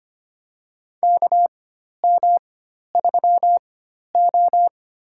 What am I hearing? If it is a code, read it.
Morse code, heard as a keyed tone, about 25 wpm, spaced as for 12 wpm, 700 Hz. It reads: KM3O
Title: USA Call Signs - 4x Character Spacing - 25wpm